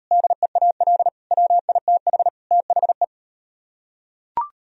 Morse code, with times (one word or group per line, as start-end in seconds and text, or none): 0.1-1.1 DEAL
1.3-2.3 WITH
2.5-3.0 THE